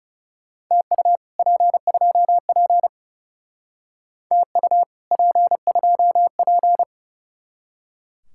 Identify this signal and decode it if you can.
Morse code, heard as a keyed tone, about 35 wpm, 700 Hz. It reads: TU P2P TU P2P
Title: Speed-Racing Common QSO Phrases Encore - 30wpm